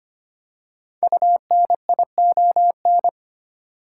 Morse code, heard as a keyed tone, about 25 wpm, 700 Hz. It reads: UNION
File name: 5 Letter Words - First 3 Letters Given - 25wpm